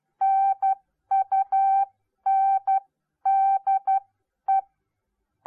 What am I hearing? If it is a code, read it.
Morse code, heard as a keyed tone, about 12 wpm, spaced as for 10 wpm, 780 Hz. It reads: NUNDE